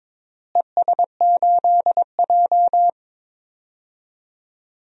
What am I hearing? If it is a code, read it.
Morse code, heard as a keyed tone, about 22 wpm, 700 Hz. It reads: ES8J
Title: DXCC-100 Call Signs - 4 Characters - Repeated 2x - 22wpm